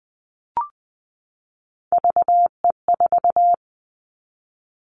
Morse code, heard as a keyed tone, about 20 wpm, 700 Hz. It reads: VE4
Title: Callsign Prefixes for Top 50 DXCC - 20wpm